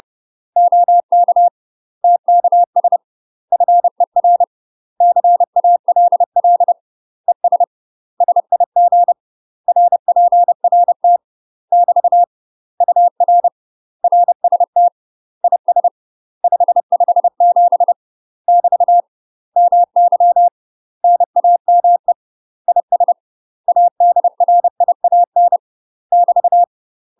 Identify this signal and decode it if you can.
Morse code, heard as a keyed tone, about 30 wpm, 700 Hz. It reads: OK TKS FER CALL ES SIG RPRT <BT> UR RST IS 557 <BT> MY NAME IS ADRIAN <BT>
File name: UK MASTER QSOs TRAINING / 30WPM / UK MASTER QSOs 0028